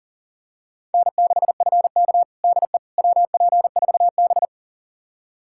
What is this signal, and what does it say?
Morse code, heard as a keyed tone, about 40 wpm, 700 Hz. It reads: N6FK DE WP4B
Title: USA Callsign DE Callsign - 4 Characters - 40wpm